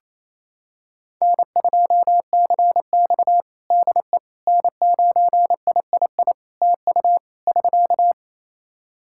Morse code, heard as a keyed tone, about 28 wpm, 700 Hz. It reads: N2CX DE N9III TU <SK>
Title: K1USN SST QSO Practice 28wpm